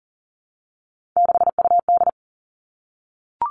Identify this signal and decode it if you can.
Morse code, heard as a keyed tone, about 38 wpm, 700 Hz. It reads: 6UD
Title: Three Challenging Letter-Number Combinations 40wpm